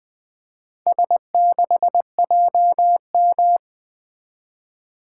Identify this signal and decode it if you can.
Morse code, heard as a keyed tone, about 20 wpm, 700 Hz. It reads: S6JM
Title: Four Letter-Number Combinations - 20wpm